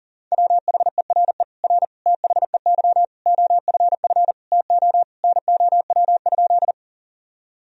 Morse code, heard as a keyed tone, about 37 wpm, 700 Hz. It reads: WHERE R THEY OFF TO NOW?
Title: Sentences from Top 200 Words and Abbreviations 40wpm